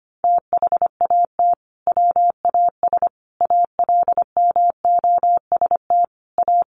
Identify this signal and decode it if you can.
Morse code, heard as a keyed tone, about 25 wpm, 700 Hz. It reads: THAT WAS ALMOST A
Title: Sentences from Top 1400 Words 25wpm